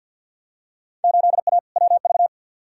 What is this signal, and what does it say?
Morse code, heard as a keyed tone, about 50 wpm, 700 Hz. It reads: 8A WV
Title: Field Day Exchanges Encore - 50wpm